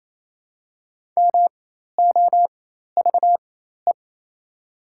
Morse code, heard as a keyed tone, about 28 wpm, spaced as for 13 wpm, 700 Hz. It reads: MOVE
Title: 4x Character Spacing - Top 500 Words - 28wpm